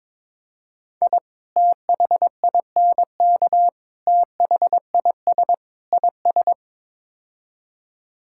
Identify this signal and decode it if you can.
Morse code, heard as a keyed tone, about 22 wpm, 700 Hz. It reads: I THINK THIS IS
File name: practice / 22wpm / Build A Phrase - 5-Word Phrases - 22wpm